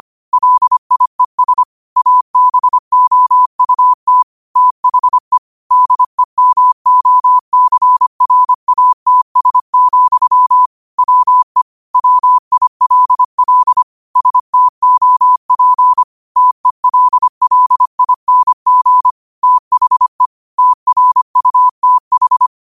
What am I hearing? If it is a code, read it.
Morse code, heard as a keyed tone, about 25 wpm, 1000 Hz. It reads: LIES ABOUT THE DEMOCRATS, WE WILL STOP TELLING THE TRUTH
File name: qotdfiles / 25wpm